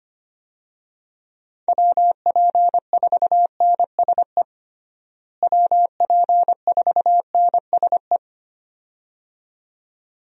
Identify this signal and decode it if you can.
Morse code, heard as a keyed tone, about 25 wpm, 700 Hz. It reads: WP4NSE WP4NSE
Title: USA Call Signs - 6 Characters - Repeated 2x - 25wpm